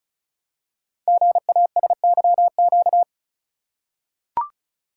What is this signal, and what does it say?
Morse code, heard as a keyed tone, about 36 wpm, 700 Hz. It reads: GASYQ
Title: Code Groups - 5 Letter:Numbers - 35wpm